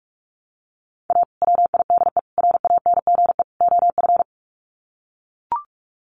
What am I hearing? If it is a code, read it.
Morse code, heard as a keyed tone, about 44 wpm, 700 Hz. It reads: A WIDE RANGE OF
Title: Sets of 4 Words 45wpm